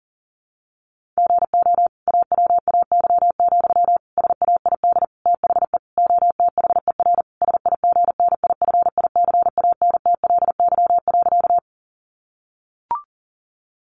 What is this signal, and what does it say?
Morse code, heard as a keyed tone, about 40 wpm, 700 Hz. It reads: GO AWAY, SAID THE OTHER SIGNIFICANTLY.
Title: Sentences from Top 1500 Words 40wpm